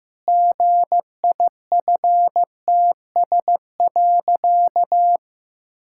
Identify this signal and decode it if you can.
Morse code, heard as a keyed tone, about 16 wpm, 700 Hz. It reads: GIFTS.
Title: Pangram Challenge v2 - 15wpm